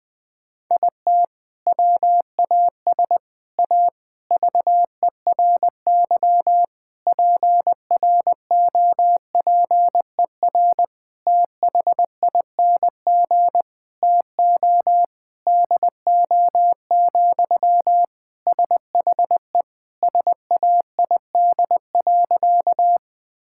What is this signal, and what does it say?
Morse code, heard as a keyed tone, about 21 wpm, 700 Hz. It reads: IT WAS A VERY PROPER THING TO DO, SHE SAID.
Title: Sentences from Top 1000 Words - Review Part 5 - 20wpm